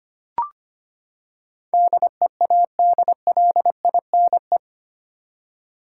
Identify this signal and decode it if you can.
Morse code, heard as a keyed tone, about 25 wpm, 700 Hz. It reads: DEADLINE
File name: Eight Letter Words - 25wpm